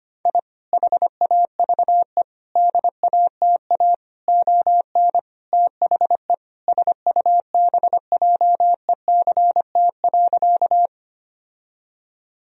Morse code, heard as a keyed tone, about 25 wpm, 700 Hz. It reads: I HAVE DATA ON THE SUBJECT.
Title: Sentences from Top 1000 Words - Review Part 3 - 25wpm